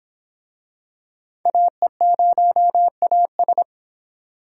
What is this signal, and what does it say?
Morse code, heard as a keyed tone, about 25 wpm, 700 Hz. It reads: AE0AS